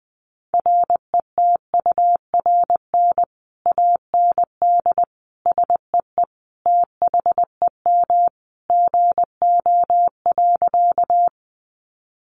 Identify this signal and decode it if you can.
Morse code, heard as a keyed tone, about 19 wpm, 700 Hz. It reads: RETURN AND SEE THEM GO.
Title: Sentences from Top 800 Words 20wpm